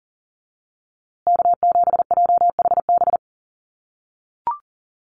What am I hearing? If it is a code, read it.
Morse code, heard as a keyed tone, about 40 wpm, 700 Hz. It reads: K7JHB
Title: USA Call Signs Repeated 2x 40wpm